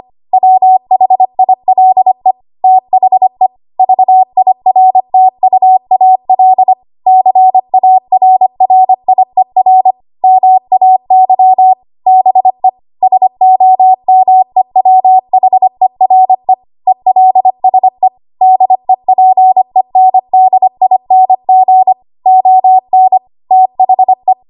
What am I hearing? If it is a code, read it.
Morse code, heard as a keyed tone, about 25 wpm, 750 Hz. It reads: WHILE THE VIRTUAL CARRIER MAY BE SOMEWHERE ELSE DEPENDING ON THE
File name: Archive / 25 WPM / 25WPM